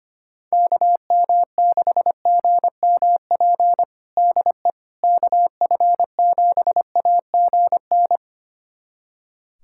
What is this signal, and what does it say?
Morse code, heard as a keyed tone, about 25 wpm, 700 Hz. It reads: KM6GMP DE KF7AGN
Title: USA Callsign DE Callsign - 6 Characters - 25wpm